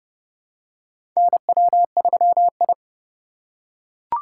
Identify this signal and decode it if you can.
Morse code, heard as a keyed tone, about 31 wpm, 700 Hz. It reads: NW3I
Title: USA Call Signs - 4 Characters - 3x Character Spacing - 30wpm